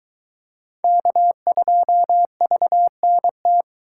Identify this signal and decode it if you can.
Morse code, heard as a keyed tone, about 22 wpm, 700 Hz. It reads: K2VNT